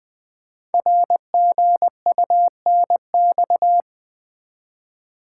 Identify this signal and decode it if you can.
Morse code, heard as a keyed tone, about 20 wpm, 700 Hz. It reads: RGUNX
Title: Code Groups - 5 Letters - 20wpm